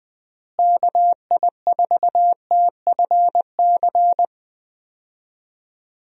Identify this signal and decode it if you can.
Morse code, heard as a keyed tone, about 20 wpm, 700 Hz. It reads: KI4TFC
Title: USA Call Signs - 6 Characters - Repeated 3x - 20wpm